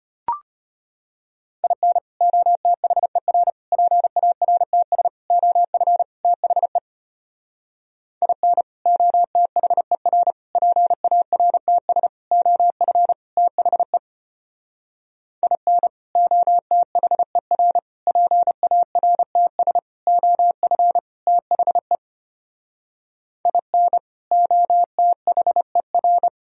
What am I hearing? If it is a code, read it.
Morse code, heard as a keyed tone, about 32 wpm, 700 Hz. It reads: IN OTHER PARTS OF THE IN OTHER PARTS OF THE IN OTHER PARTS OF THE IN OTHER